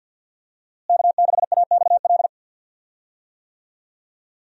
Morse code, heard as a keyed tone, about 52 wpm, 700 Hz. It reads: K6AXL
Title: USA Call Signs - 5 Characters - Repeated 3x - 50wpm